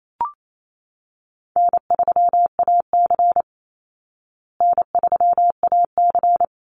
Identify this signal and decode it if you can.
Morse code, heard as a keyed tone, about 28 wpm, 700 Hz. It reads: N3AC N3AC
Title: Easy Call Signs Repeated 3x - 28wpm